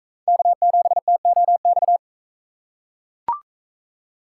Morse code, heard as a keyed tone, about 41 wpm, 700 Hz. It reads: K7TOX